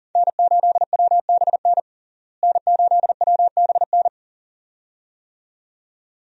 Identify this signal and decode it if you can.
Morse code, heard as a keyed tone, about 40 wpm, 700 Hz. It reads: N8WBN N8WBN
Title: USA Call Signs - 5 Characters - Repeated 2x - 40wpm